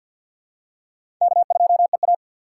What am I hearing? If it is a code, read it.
Morse code, heard as a keyed tone, about 52 wpm, 700 Hz. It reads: K2EA
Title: USA Call Signs - 4 Characters - 2x Character Spacing - 50wpm